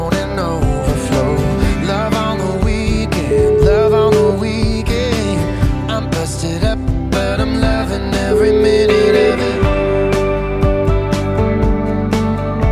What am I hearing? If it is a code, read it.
Ringback tone (Europe, 425 Hz), with noise.